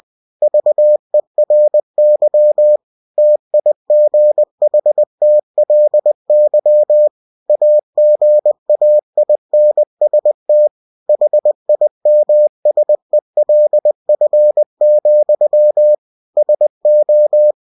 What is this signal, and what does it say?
Morse code, heard as a keyed tone, about 20 wpm, 600 Hz. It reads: VERY TIGHTLY AGAINST HIMSELF, SO